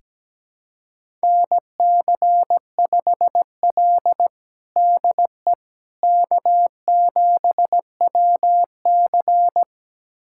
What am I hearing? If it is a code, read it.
Morse code, heard as a keyed tone, about 17 wpm, 700 Hz. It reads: NC5L DE K7WC
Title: USA Callsign DE Callsign - 4 Characters - 17wpm